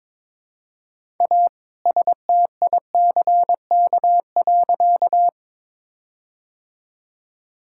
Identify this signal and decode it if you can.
Morse code, heard as a keyed tone, about 22 wpm, 700 Hz. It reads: A STICK.
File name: All-Age Appropriate Jokes - 22wpm